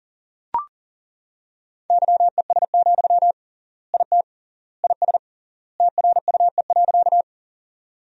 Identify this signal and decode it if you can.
Morse code, heard as a keyed tone, about 37 wpm, 700 Hz. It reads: YES, IT IS TRUE.
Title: 3x Word-Spacing - Sentences from Top 500 Words - 40wpm